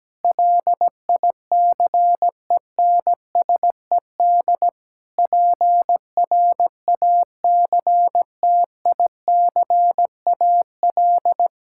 Morse code, heard as a keyed tone, about 17 wpm, 700 Hz. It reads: LICENSED PRACTICAL
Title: Practice QSO - Exchange 3 - More Info - 17wpm